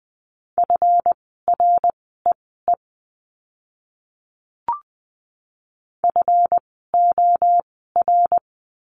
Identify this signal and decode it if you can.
Morse code, heard as a keyed tone, about 20 wpm, spaced as for 14 wpm, 700 Hz. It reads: FREE FOR
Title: 139 Lesson 23f - Letter f - Words 20wpm